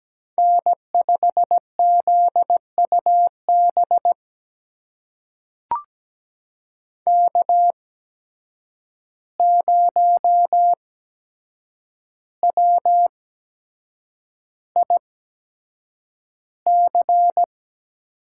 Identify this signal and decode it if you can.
Morse code, heard as a keyed tone, about 17 wpm, spaced as for 7 wpm, 700 Hz. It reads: N5ZUB K0WIC
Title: USA Call Signs - 5 Characters - 8x Character Spacing - 17wpm